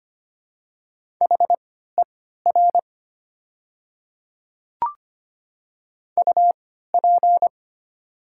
Morse code, heard as a keyed tone, about 25 wpm, spaced as for 14 wpm, 700 Hz. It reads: HER UP
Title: 3x Character Spacing - Top 300 Words - 25wpm